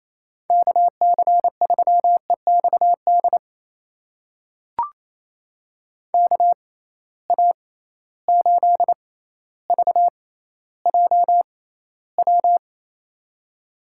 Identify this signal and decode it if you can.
Morse code, heard as a keyed tone, about 28 wpm, spaced as for 14 wpm, 700 Hz. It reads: KC3EXD KA8VJW
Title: USA Call Signs - 6 Characters - 6x Character Spacing - 28wpm